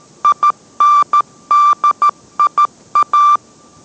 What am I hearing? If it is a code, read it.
Morse code, heard as a keyed tone, about 13 wpm, 1180 Hz. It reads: INDIA